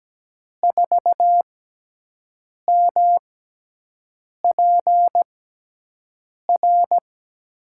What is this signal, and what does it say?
Morse code, heard as a keyed tone, about 17 wpm, spaced as for 6 wpm, 700 Hz. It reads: 4MPR
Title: USA Call Signs - 5 Characters - 6x Character Spacing - 17wpm